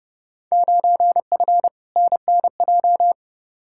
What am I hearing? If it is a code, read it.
Morse code, heard as a keyed tone, about 30 wpm, 700 Hz. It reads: 9F NNJ
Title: Field Day Exchanges Encore - 30wpm